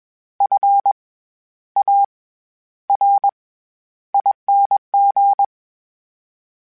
Morse code, heard as a keyed tone, about 21 wpm, 800 Hz. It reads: F A R ING
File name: Special INGs-mp3s / Special INGs-mp3s / SlowToFaster ing @20